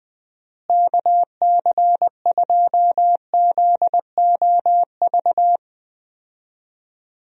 Morse code, heard as a keyed tone, about 20 wpm, 700 Hz. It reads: KC2ZOV
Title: USA Call Signs - 6 Characters - Repeated 2x - 20wpm